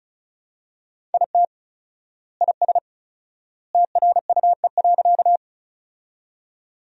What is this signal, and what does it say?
Morse code, heard as a keyed tone, about 36 wpm, 700 Hz. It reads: IT IS TRUE.
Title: 4x Word-Spacing - Sentences from Top 400 Words - 35wpm